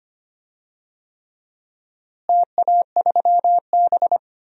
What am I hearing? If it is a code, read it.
Morse code, heard as a keyed tone, about 25 wpm, 700 Hz. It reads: TA3B
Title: DXCC-100 Call Signs - 4 Characters - Repeated 3x - 25wpm